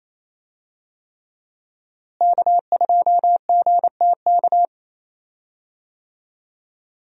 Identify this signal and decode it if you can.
Morse code, heard as a keyed tone, about 28 wpm, 700 Hz. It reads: K2GTK
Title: USA Call Signs - 5 Characters - Repeated 3x - 28wpm